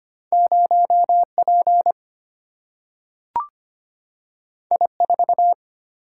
Morse code, heard as a keyed tone, about 25 wpm, 700 Hz. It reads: 0 P I 4
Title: Single-Letter Single-Number 25wpm